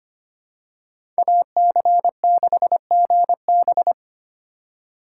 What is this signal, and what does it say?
Morse code, heard as a keyed tone, about 25 wpm, 700 Hz. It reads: AC6GB
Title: USA Call Signs - 5 Characters - 2x Character Spacing - 25wpm